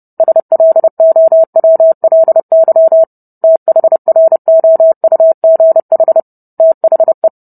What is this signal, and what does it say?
Morse code, heard as a keyed tone, about 30 wpm, 650 Hz. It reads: SLOWLY THROUGH THE